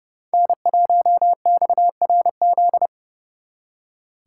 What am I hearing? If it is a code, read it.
Morse code, heard as a keyed tone, about 37 wpm, 700 Hz. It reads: N1XRZ
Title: Speed Racing - USA 1x3 Call Signs 30wpm